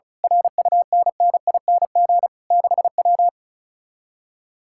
Morse code, heard as a keyed tone, about 33 wpm, 700 Hz. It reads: RUNNING 6W
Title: Speed-Racing Common QSO Phrases - 35wpm